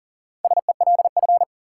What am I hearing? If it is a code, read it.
Morse code, heard as a keyed tone, about 40 wpm, 700 Hz. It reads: SELF
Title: Sentences from Top 1000 Words - Review Part 1 - New Word Drill - 40wpm